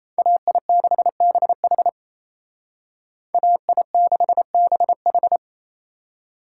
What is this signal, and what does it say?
Morse code, heard as a keyed tone, about 36 wpm, 700 Hz. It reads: AI6BH AI6BH